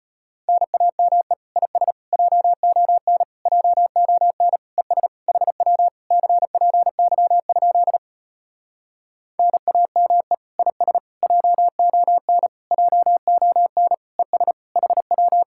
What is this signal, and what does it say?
Morse code, heard as a keyed tone, about 33 wpm, 700 Hz. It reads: NAME IS JON JON ES HW CPY? NAME IS JON JON ES HW